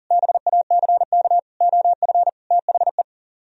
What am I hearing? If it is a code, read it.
Morse code, heard as a keyed tone, about 52 wpm, 700 Hz. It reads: BACK OF THE